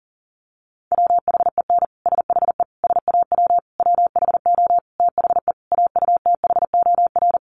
Morse code, heard as a keyed tone, about 40 wpm, 700 Hz. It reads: WHEN SHE SAW WHO THE AUTHOR
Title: Sentences from Top 1200 Words 40wpm